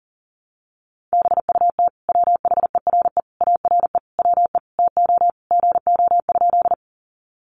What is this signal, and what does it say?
Morse code, heard as a keyed tone, about 40 wpm, 700 Hz. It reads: BUT WHERE ARE WE TO GO?
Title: Sentences from Top 200 Words 40wpm